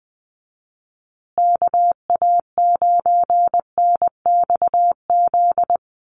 Morse code, heard as a keyed tone, about 20 wpm, 700 Hz. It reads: KA9NXZ
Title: USA Call Signs Repeated 2x 20wpm